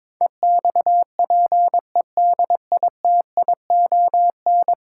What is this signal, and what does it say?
Morse code, heard as a keyed tone, about 22 wpm, 700 Hz. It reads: EXPEDITION
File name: practice / 22wpm / 10 Letter Words - First 3 Letters Given - 22wpm